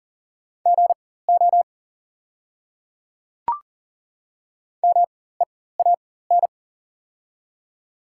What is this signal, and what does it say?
Morse code, heard as a keyed tone, about 37 wpm, spaced as for 18 wpm, 700 Hz. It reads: GO MEAN
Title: 4x Character Spacing - Top 300 Words - 40wpm